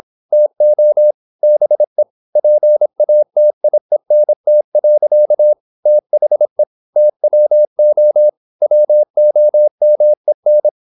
Morse code, heard as a keyed tone, about 26 wpm, 600 Hz. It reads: TO BE PATIENT. THE TWO WOMEN
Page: Never 26wpm | CW med Gnister